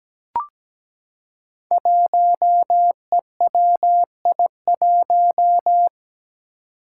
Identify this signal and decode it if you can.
Morse code, heard as a keyed tone, about 17 wpm, 700 Hz. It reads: 1EWI1
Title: Five Challenging Letter-Number Combinations 17wpm